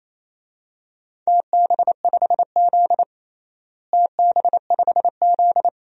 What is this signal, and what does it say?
Morse code, heard as a keyed tone, about 28 wpm, 700 Hz. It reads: TB5Z TB5Z
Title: DXCC-100 Call Signs - 4 Characters - Repeated 2x - 28wpm